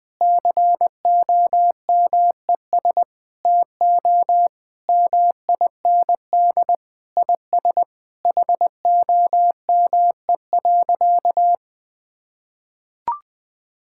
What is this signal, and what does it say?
Morse code, heard as a keyed tone, about 20 wpm, 700 Hz. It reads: COMES TO MIND IS HOME.
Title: Sentences with 7-Word Phrases - 20wpm